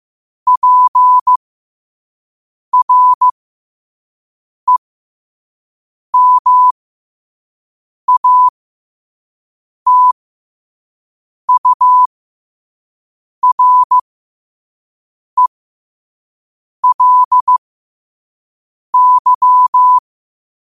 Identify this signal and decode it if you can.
Morse code, heard as a keyed tone, about 15 wpm, spaced as for 5 wpm, 1000 Hz. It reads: PREMATURELY
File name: qotdfiles / 5wpm